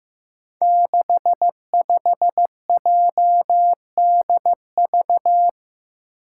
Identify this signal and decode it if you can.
Morse code, heard as a keyed tone, about 15 wpm, 700 Hz. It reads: 65JDV
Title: Five Challenging Letter-Number Combinations 15wpm